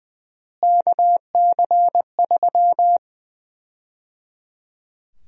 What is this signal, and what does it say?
Morse code, heard as a keyed tone, about 20 wpm, 700 Hz. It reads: KC3
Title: USA Call Sign Prefixes - Repeated 2x - 20wpm